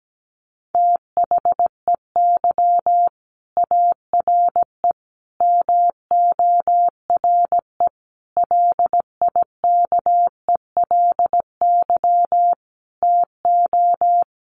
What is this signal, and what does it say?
Morse code, heard as a keyed tone, about 17 wpm, 700 Hz. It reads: THEY ARE MORE LIKELY TO
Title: Sets of 5 Words Encore2 17wpm